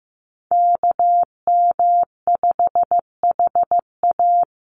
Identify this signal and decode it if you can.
Morse code, heard as a keyed tone, about 15 wpm, 700 Hz. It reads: KM5HA
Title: USA Call Signs 15wpm